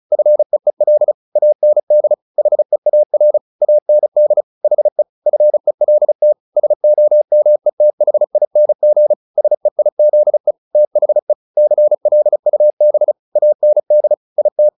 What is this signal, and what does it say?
Morse code, heard as a keyed tone, about 35 wpm, 600 Hz. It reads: FEEL AND HEAR AND HE FELT SOMETHING SEIZE THE CLUB AND IT
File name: the old man and the sea 2 35wpm